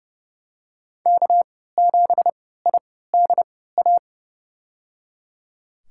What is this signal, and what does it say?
Morse code, heard as a keyed tone, about 30 wpm, spaced as for 17 wpm, 700 Hz. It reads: K7IDA
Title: USA Call Signs - 3x Character Spacing - 30wpm